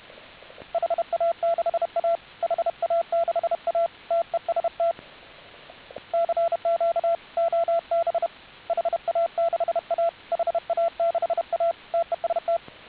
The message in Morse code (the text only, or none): HA6A HA6A TEST CQ OB HA6A HA6A TEST